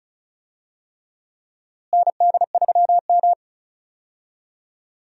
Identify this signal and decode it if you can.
Morse code, heard as a keyed tone, about 36 wpm, 700 Hz. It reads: ND3M
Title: USA Call Signs - 4 Characters - Repeated 3x - 35wpm